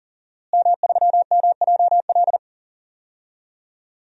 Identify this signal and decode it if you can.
Morse code, heard as a keyed tone, about 39 wpm, 700 Hz. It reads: M3MJL
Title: DXCC-100 Call Signs - 5 Characters - Repeated 2x - 40wpm